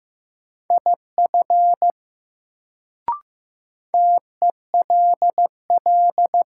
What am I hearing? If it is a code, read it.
Morse code, heard as a keyed tone, about 15 wpm, 700 Hz. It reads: IF TELL
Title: Sets of 2 Words - New Word Drill - 15wpm